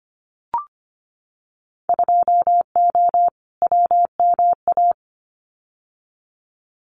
Morse code, heard as a keyed tone, about 25 wpm, 700 Hz. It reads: 2O WMA
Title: Winter Field Day Exchanges Encore 25wpm